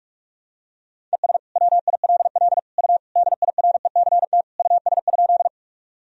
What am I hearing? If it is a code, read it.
Morse code, heard as a keyed tone, about 41 wpm, 700 Hz. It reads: ES WILL U DIRECT US?
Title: Sentences from Top 400 Words and Abbreviations 45wpm